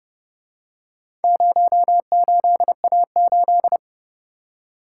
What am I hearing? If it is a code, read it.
Morse code, heard as a keyed tone, about 31 wpm, 700 Hz. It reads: 08A8